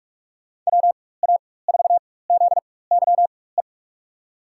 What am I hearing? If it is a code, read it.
Morse code, heard as a keyed tone, about 42 wpm, spaced as for 21 wpm, 700 Hz. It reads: WA4ZYE
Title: USA Call Signs - 4x Character Spacing - 45wpm